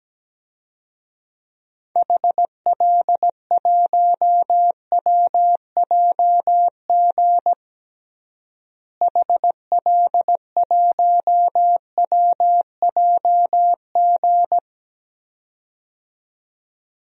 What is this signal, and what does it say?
Morse code, heard as a keyed tone, about 17 wpm, 700 Hz. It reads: HL1WJG HL1WJG
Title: DXCC-100 Call Signs - 6 Characters - Repeated 2x - 17wpm